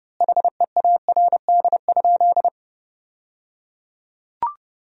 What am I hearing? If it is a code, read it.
Morse code, heard as a keyed tone, about 30 wpm, 700 Hz. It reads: HEARD?
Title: 3x Word-Spacing - Sentences from Top 400 Words - 30wpm